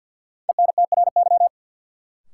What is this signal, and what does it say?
Morse code, heard as a keyed tone, about 47 wpm, 700 Hz. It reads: ENTRY
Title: Sentences from Top 1500 Words - New Word Drill - 50wpm